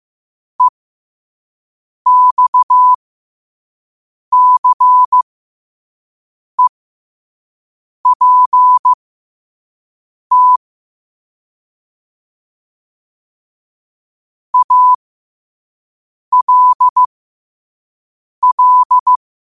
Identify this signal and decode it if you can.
Morse code, heard as a keyed tone, about 15 wpm, spaced as for 5 wpm, 1000 Hz. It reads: EXCEPT ALL
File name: qotdfiles / 5wpm